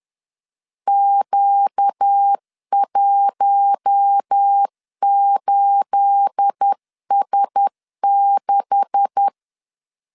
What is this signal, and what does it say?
Morse code, heard as a keyed tone, about 10 wpm, 800 Hz. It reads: Q18S6
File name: LEZIONI / 9 286 10wpm 2735